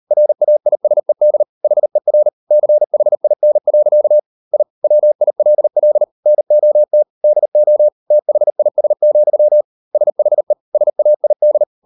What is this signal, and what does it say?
Morse code, heard as a keyed tone, about 38 wpm, 600 Hz. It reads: RAISED HER CHIN. I WILL NOT DO THIS, SHE SAID